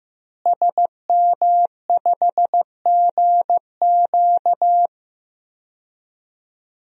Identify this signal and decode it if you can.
Morse code, heard as a keyed tone, about 15 wpm, 700 Hz. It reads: SM5GQ
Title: DXCC-100 Call Signs - 5 Characters - Repeated 3x - 15wpm